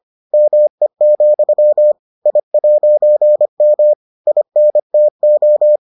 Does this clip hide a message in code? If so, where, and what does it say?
Yes: Morse code from 0.3 to 5.8 s, heard as a keyed tone, about 25 wpm, 600 Hz. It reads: ME, I'M INTO